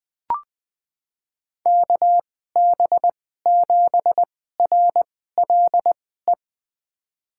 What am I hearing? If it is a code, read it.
Morse code, heard as a keyed tone, about 20 wpm, spaced as for 15 wpm, 700 Hz. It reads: KB7RLE
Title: USA Call Signs - 6 Characters - 2x Character Spacing - 20wpm